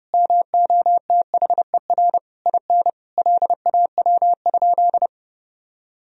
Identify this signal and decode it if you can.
Morse code, heard as a keyed tone, about 28 wpm, 700 Hz. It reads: MOTHER IN LAW?
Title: Sentences from Top 1000 Words - Review Part 4 - 30wpm